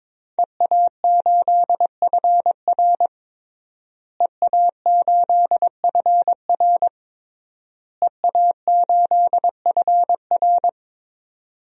Morse code, heard as a keyed tone, about 22 wpm, 700 Hz. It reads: EA8FR EA8FR EA8FR
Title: DXCC-100 Call Signs - 5 Characters - Repeated 3x - 22wpm